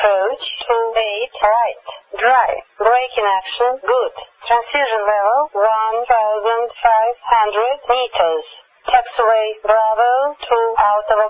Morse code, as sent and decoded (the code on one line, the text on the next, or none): none